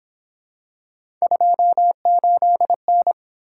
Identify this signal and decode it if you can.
Morse code, heard as a keyed tone, about 27 wpm, 700 Hz. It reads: 28N
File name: Speed-Racing - Three Letter-Number Combinations - 22wpm